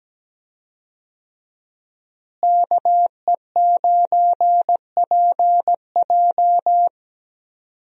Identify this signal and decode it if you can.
Morse code, heard as a keyed tone, about 17 wpm, 700 Hz. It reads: KE9PJ
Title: USA Call Signs - 5 Characters - Repeated 2x - 17wpm